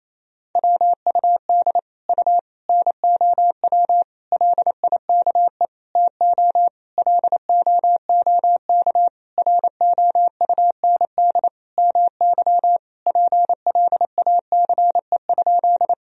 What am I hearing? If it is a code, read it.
Morse code, heard as a keyed tone, about 28 wpm, 700 Hz. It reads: WUD U NOW LIKE TO LOOK ROUND MY PLACE?
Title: Sentences from Top 200 Words and Abbreviations 28wpm